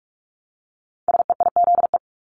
Morse code, heard as a keyed tone, about 46 wpm, 690 Hz. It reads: SEIZE
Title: Multi-letter Phonemes Repeated 3x 45wpm